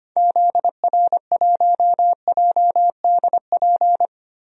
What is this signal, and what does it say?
Morse code, heard as a keyed tone, about 25 wpm, 700 Hz. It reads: ZR1JDP
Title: DXCC-100 Call Signs - 6 Characters - Repeated 2x - 25wpm